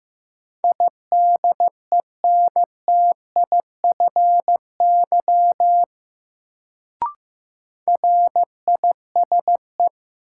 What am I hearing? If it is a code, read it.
Morse code, heard as a keyed tone, about 15 wpm, 700 Hz. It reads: IDENTIFY RISE
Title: Sentences from Top 900 Words - New Word Drill - 15wpm